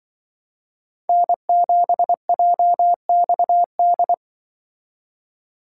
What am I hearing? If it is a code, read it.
Morse code, heard as a keyed tone, about 24 wpm, 700 Hz. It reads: N7JXD
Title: Speed Racing - USA 1x3 Call Signs 20wpm